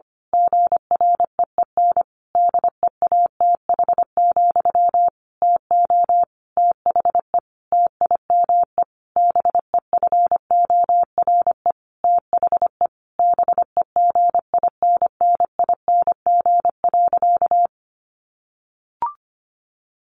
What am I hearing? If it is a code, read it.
Morse code, heard as a keyed tone, about 25 wpm, 700 Hz. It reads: GREEN DEATH, TO THE TIME BEFORE THE BEGINNING.
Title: A Sound Of Thunder 25wpm